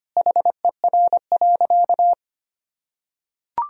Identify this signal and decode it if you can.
Morse code, heard as a keyed tone, about 25 wpm, 700 Hz. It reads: HER.
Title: Sentences from Top 1000 Words - Review Part 4 - 25wpm